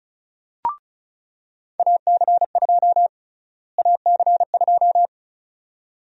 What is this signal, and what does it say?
Morse code, heard as a keyed tone, about 36 wpm, 700 Hz. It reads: AC2 AC2
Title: USA Call Sign Prefixes - Repeated 2x - 35wpm